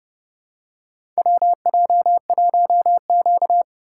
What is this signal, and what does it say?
Morse code, heard as a keyed tone, about 30 wpm, 700 Hz. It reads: WJ1Q
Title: USA Call Signs - 4 Characters - 3x Character Spacing - 30wpm